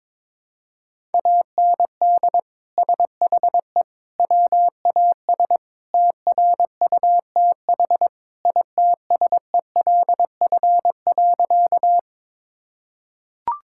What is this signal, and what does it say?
Morse code, heard as a keyed tone, about 22 wpm, 700 Hz. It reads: AND SHE WAS TRUTH ITSELF.
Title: Sentences from Top 1000 Words - Review Part 5 - 22wpm